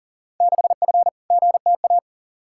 Morse code, heard as a keyed tone, about 40 wpm, 700 Hz. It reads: 6F GTA
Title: Field Day Exchanges Encore - 40wpm